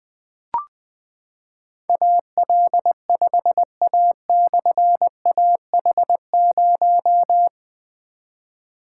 Morse code, heard as a keyed tone, about 20 wpm, 700 Hz. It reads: AL5A/AH0
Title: Difficult Call Signs 20wpm